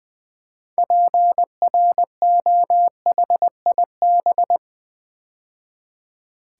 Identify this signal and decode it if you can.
Morse code, heard as a keyed tone, about 20 wpm, 700 Hz. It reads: PROHIB
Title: Build A Word - 2 Letters At A Time - 8-Letter Words - 20wpm